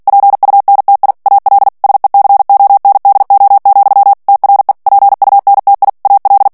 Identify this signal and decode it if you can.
Morse code, heard as a keyed tone, about 48 wpm, 800 Hz. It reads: PUNTI AL SECONDO, TRE PUNTI AL
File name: ITALIEN texte clair 2 vitesse 45 mots minute